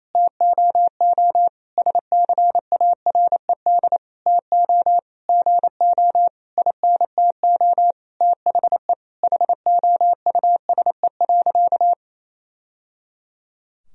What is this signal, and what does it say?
Morse code, heard as a keyed tone, about 28 wpm, 700 Hz. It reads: TOO SCARED TO GO INTO THE HOUSE.
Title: Sentences from Top 1000 Words - Review Part 4 - 28wpm